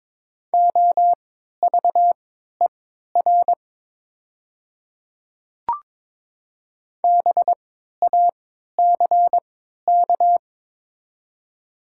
Morse code, heard as a keyed tone, about 21 wpm, spaced as for 12 wpm, 690 Hz. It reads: OVER BACK
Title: 3x Character Spacing - Top 200 Words - 22wpm